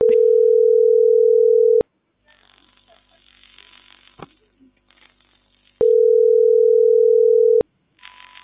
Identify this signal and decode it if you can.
Ringback tone (North America), 440 + 480 Hz.